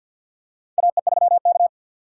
Morse code, heard as a keyed tone, about 47 wpm, 700 Hz. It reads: AE3K
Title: USA Call Signs - 4 Characters - 2x Character Spacing - 50wpm